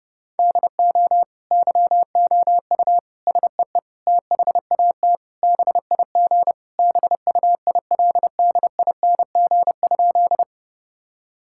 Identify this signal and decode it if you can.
Morse code, heard as a keyed tone, about 32 wpm, 700 Hz. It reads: DO YOU SEE THAT BIG BUILDING?
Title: Sentences from Top 1000 Words - Review Part 5 - 30wpm